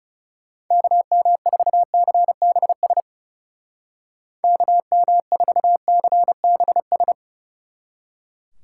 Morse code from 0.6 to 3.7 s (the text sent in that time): KM4CBS